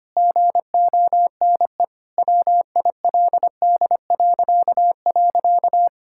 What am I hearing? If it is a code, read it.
Morse code, heard as a keyed tone, about 25 wpm, 700 Hz. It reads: GONE WILD..
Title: Pangram Challenge v2 - 25wpm